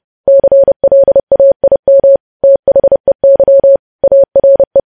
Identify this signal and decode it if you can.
Morse code, heard as a keyed tone, about 30 wpm, 550 Hz. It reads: CLAIM THEY ARE
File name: podcast / News headlines-1x-30x30